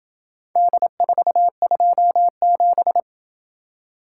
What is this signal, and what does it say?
Morse code, heard as a keyed tone, about 28 wpm, 700 Hz. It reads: D427